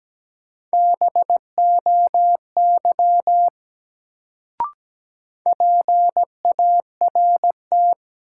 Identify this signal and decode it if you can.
Morse code, heard as a keyed tone, about 17 wpm, 700 Hz. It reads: BOY PART
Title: Sentences from Top 200 Words - New Word Drill - 17wpm